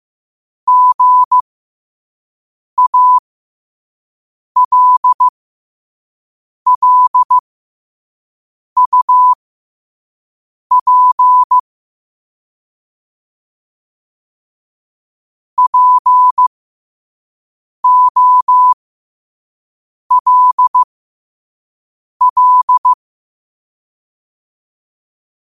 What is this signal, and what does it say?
Morse code, heard as a keyed tone, about 15 wpm, spaced as for 5 wpm, 1000 Hz. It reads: GALLUP POLL
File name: qotdfiles / 5wpm